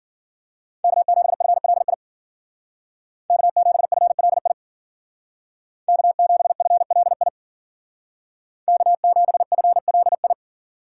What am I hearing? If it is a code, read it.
Morse code, heard as a keyed tone, about 49 wpm, 700 Hz. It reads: K7FLI K7FLI K7FLI K7FLI
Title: Speed Racing - USA 1x3 Call Signs 40wpm